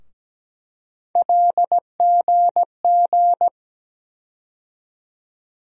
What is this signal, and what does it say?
Morse code, heard as a keyed tone, about 17 wpm, 700 Hz. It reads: LGG
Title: USA Call Sign Postfixes - 3 Letters 17wpm